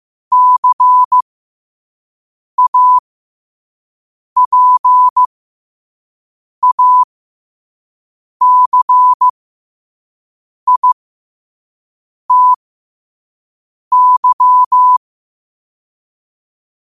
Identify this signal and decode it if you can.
Morse code, heard as a keyed tone, about 15 wpm, spaced as for 5 wpm, 1000 Hz. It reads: CAPACITY
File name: qotdfiles / 5wpm